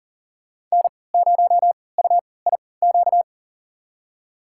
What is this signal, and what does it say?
Morse code, heard as a keyed tone, about 40 wpm, spaced as for 23 wpm, 690 Hz. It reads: N0UIQ
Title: USA Call Signs - 5 Characters - 3x Character Spacing - 40wpm